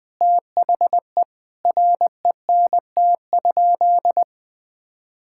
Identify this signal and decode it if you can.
Morse code, heard as a keyed tone, about 20 wpm, 700 Hz. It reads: THE RENT?
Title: Sentences from Top 1000 Words - Review Part 5 - 20wpm